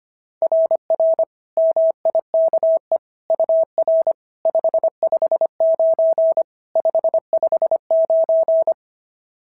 Morse code, heard as a keyed tone, about 23 wpm, 650 Hz. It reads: RR MIKE UR 559 559
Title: SOTA QSO Practice 25wpm